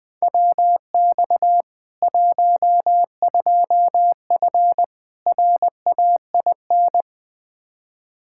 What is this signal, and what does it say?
Morse code, heard as a keyed tone, about 20 wpm, 700 Hz. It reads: WX 12F RAIN
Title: K1USN SST QSO Practice 20wpm